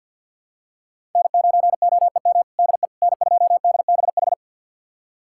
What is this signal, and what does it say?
Morse code, heard as a keyed tone, about 51 wpm, 700 Hz. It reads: N9OEM DE N2DBH
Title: USA Callsign DE Callsign - 5 Characters - 50wpm